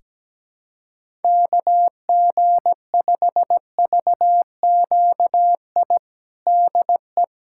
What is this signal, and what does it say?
Morse code, heard as a keyed tone, about 17 wpm, 700 Hz. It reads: KG5VQI DE
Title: USA Callsign DE Callsign 17wpm